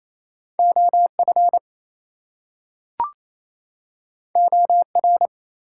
Morse code, heard as a keyed tone, about 27 wpm, 700 Hz. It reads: OF OR
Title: Two Letter Words - 28wpm